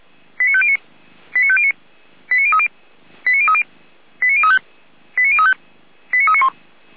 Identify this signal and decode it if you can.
Five-tone selective call (ZVEI), heard as quick runs of five tones, tones of about 70 ms. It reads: 89509 89509 89030 89030 89035 89035 89391
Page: SDR/ZVEI